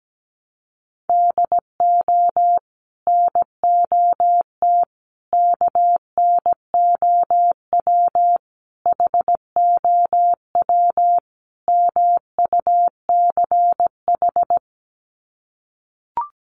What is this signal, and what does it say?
Morse code, heard as a keyed tone, about 17 wpm, 700 Hz. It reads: DO NOT KNOW HOW MUCH
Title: Sets of 5 Words Encore 17wpm